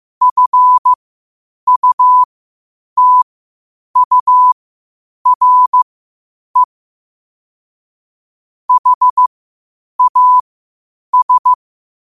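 Morse code, heard as a keyed tone, about 15 wpm, spaced as for 8 wpm, 1000 Hz. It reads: FUTURE HAS
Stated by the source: Morse practice at 8 WPM from Morse Resource.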